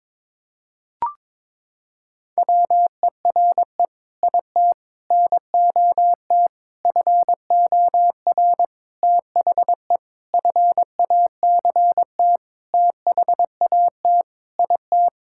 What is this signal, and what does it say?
Morse code, heard as a keyed tone, about 22 wpm, 700 Hz. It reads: WERE IT NOT FOR THE FACT THAT IT
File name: Sentences with 7-Word Phrases - 22wpm